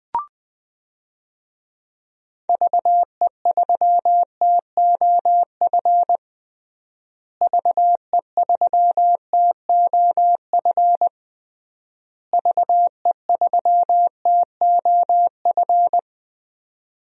Morse code, heard as a keyed tone, about 20 wpm, 700 Hz. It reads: VE3TOF VE3TOF VE3TOF
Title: DXCC-100 Call Signs - 6 Characters - Repeated 3x - 20wpm